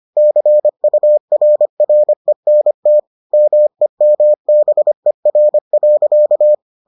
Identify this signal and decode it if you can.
Morse code, heard as a keyed tone, about 25 wpm, 600 Hz. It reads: CURRENT MEMBER.